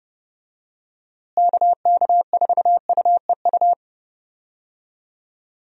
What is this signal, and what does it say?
Morse code, heard as a keyed tone, about 30 wpm, 700 Hz. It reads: KK4UEU
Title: USA Call Signs - 6 Characters - Repeated 2x - 30wpm